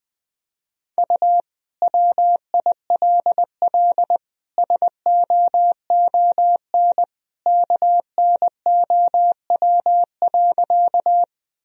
Morse code, heard as a keyed tone, about 21 wpm, 700 Hz. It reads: U WILL SOON KNOW.
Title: Sentences from Top 400 Words and Abbreviations 20wpm